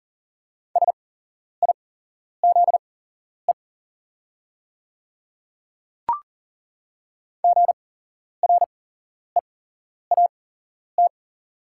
Morse code, heard as a keyed tone, about 38 wpm, spaced as for 11 wpm, 700 Hz. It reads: SIZE GREAT
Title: 8x Character Spacing - Top 500 Words - 40wpm